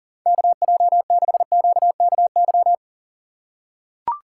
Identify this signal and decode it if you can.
Morse code, heard as a keyed tone, about 40 wpm, 700 Hz. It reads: KJ6QKY
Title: USA Call Signs - 6 Characters - 2x Character Spacing - 40wpm